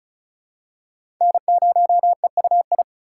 Morse code, heard as a keyed tone, about 34 wpm, 690 Hz. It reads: N0EUI
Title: USA Call Signs - 5 Characters - 2x Character Spacing - 35wpm